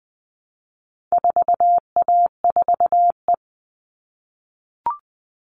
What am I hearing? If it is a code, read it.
Morse code, heard as a keyed tone, about 19 wpm, 700 Hz. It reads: 4A4E
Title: All Cut Numbers - 4-Digit 20wpm